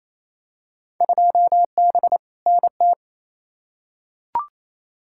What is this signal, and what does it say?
Morse code, heard as a keyed tone, about 28 wpm, 700 Hz. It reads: 2B NT
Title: Field Day Exchanges Encore - 28wpm